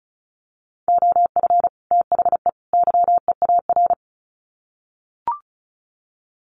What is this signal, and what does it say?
Morse code, heard as a keyed tone, about 34 wpm, 700 Hz. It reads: OF THE YEAR
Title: Sets of 3 Words 35wpm